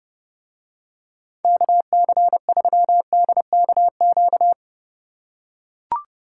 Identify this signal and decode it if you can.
Morse code, heard as a keyed tone, about 28 wpm, 700 Hz. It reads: KC3DKQ